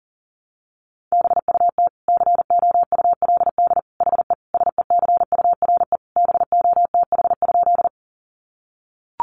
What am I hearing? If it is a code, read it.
Morse code, heard as a keyed tone, about 40 wpm, 700 Hz. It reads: BUT COULD HE SECURE BOTH?
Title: Sentences from Top 1200 Words 40wpm